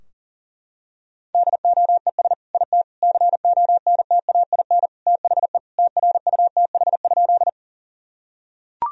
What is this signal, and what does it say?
Morse code, heard as a keyed tone, about 40 wpm, 700 Hz. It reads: DOES IT CONTAIN THE TRUTH?
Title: Sentences from Top 1000 Words - Review Part 2 - 40wpm